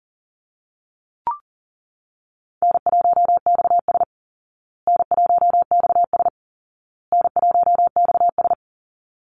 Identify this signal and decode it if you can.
Morse code, heard as a keyed tone, about 40 wpm, 700 Hz. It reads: N1XS N1XS N1XS
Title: Easy Call Signs Repeated 3x - 40wpm